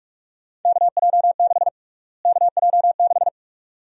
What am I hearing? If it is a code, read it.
Morse code, heard as a keyed tone, about 42 wpm, 700 Hz. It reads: KJ6 KJ6
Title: USA Call Sign Prefixes - Repeated 2x - 45wpm